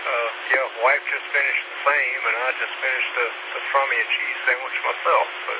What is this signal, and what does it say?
Emergency Alert System (SAME) tones, heard as harsh buzzing bursts, with speech. It is heard from 1.0 to 4.6 s.